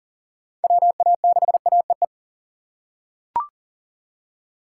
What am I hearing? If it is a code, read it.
Morse code, heard as a keyed tone, about 40 wpm, 700 Hz. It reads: WA6AEE
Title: K1USN SST QSO Practice - 40wpm